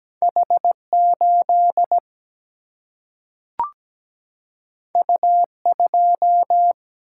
Morse code, heard as a keyed tone, about 17 wpm, 700 Hz. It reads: H8 U2
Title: Single-Letter Single-Number 17wpm